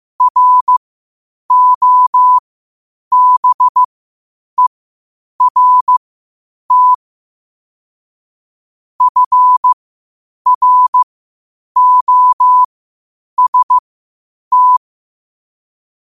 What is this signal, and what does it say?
Morse code, heard as a keyed tone, about 15 wpm, spaced as for 8 wpm, 1000 Hz. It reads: ROBERT FROST